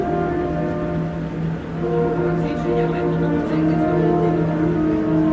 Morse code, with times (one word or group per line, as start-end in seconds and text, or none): none